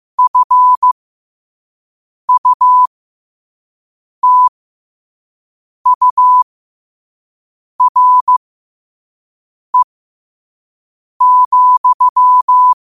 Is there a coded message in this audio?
Yes: Morse code from 0.2 to 12.7 s, heard as a keyed tone, about 15 wpm, spaced as for 5 wpm, 1000 Hz. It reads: FUTURE,